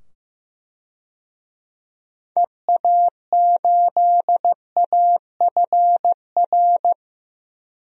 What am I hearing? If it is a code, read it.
Morse code, heard as a keyed tone, about 15 wpm, 700 Hz. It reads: EA8AFR
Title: DXCC-100 Call Signs - 6 Characters - 15wpm